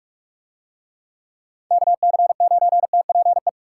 Morse code, heard as a keyed tone, about 45 wpm, 700 Hz. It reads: KC9TPE